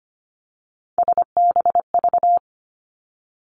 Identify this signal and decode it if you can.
Morse code, heard as a keyed tone, about 25 wpm, 700 Hz. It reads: SBV
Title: Three Challenging Letter-Number Combinations 25wpm